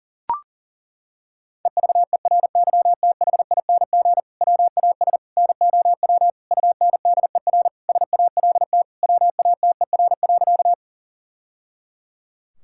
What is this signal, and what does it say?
Morse code, heard as a keyed tone, about 37 wpm, 700 Hz. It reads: EVERYTHING WAS NOW UNDER SALT WATER.
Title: Sentences from Top 1000 Words - Review Part 5 - 40wpm